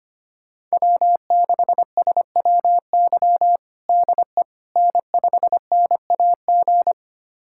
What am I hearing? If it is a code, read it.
Morse code, heard as a keyed tone, about 25 wpm, 700 Hz. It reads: W6SWY DE N5NAG
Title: USA Callsign DE Callsign 25wpm